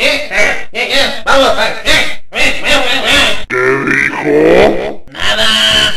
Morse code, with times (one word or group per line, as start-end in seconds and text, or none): none